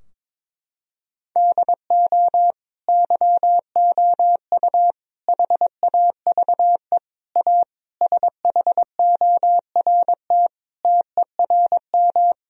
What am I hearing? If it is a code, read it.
Morse code, heard as a keyed tone, about 22 wpm, 700 Hz. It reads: DO YOU HAVE A SHORT TERM
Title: Sentences from Top 1000 Words - Review Part 4 - 22wpm